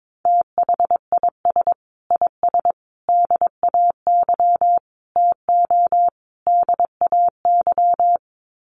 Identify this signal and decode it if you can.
Morse code, heard as a keyed tone, about 22 wpm, 700 Hz. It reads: THIS IS DAY TO DAY
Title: Sets of 5 Words Encore 22wpm